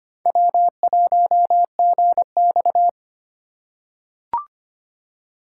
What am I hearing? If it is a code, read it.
Morse code, heard as a keyed tone, about 25 wpm, 700 Hz. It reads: W1GX
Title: USA Call Signs - 4 Characters - 3x Character Spacing - 25wpm